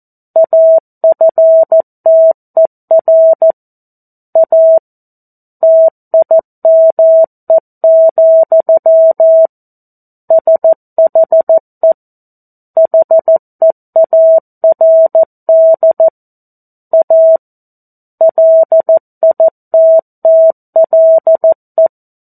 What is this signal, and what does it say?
Morse code, heard as a keyed tone, about 14 wpm, 650 Hz. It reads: AFTER A TIME, SHE HEARD A LITTLE
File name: 02-Chapter-02 14 WPM